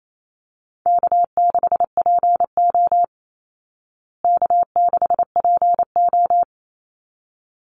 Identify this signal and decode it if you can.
Morse code, heard as a keyed tone, about 28 wpm, 700 Hz. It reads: K6PO K6PO
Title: Call Signs Repeated 3x - 28wpm